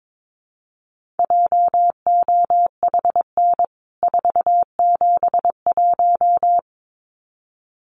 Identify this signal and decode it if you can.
Morse code, heard as a keyed tone, about 22 wpm, 700 Hz. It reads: JOHN 471
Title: CWT Exchange 22wpm